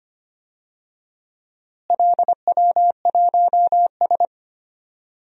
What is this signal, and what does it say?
Morse code, heard as a keyed tone, about 25 wpm, 700 Hz. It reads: LW1S
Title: DXCC-100 Call Signs - 4 Characters - Repeated 2x - 25wpm